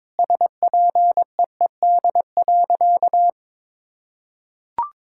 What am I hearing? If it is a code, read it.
Morse code, heard as a keyed tone, about 22 wpm, 700 Hz. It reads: SPEED.
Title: Sentences from Top 1000 Words - Review Part 5 - 22wpm